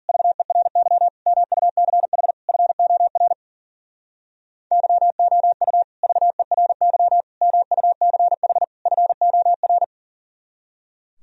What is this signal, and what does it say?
Morse code, heard as a keyed tone, about 52 wpm, 700 Hz. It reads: VERY MUCH FOR YOU VERY MUCH FOR